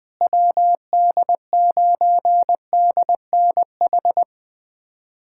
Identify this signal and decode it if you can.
Morse code, heard as a keyed tone, about 20 wpm, 700 Hz. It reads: WD9DNH
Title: USA Callsign DE Callsign - 6 Characters - 20wpm